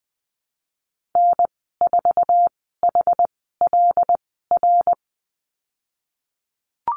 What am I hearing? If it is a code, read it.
Morse code, heard as a keyed tone, about 20 wpm, spaced as for 15 wpm, 700 Hz. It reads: N4HLR
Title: 167 Lesson 27g - Number 7 - Call Signs 20wpm